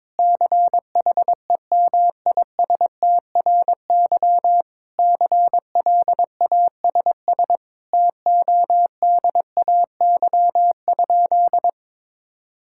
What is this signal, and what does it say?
Morse code, heard as a keyed tone, about 22 wpm, 700 Hz. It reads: CHEMISTRY CLASS TODAY?
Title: Sentences from Top 1000 Words - Review Part 6 - 22wpm